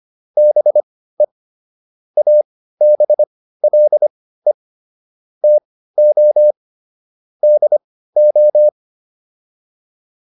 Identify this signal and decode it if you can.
Morse code, heard as a keyed tone, about 25 wpm, spaced as for 15 wpm, 600 Hz. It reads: BE ABLE TO DO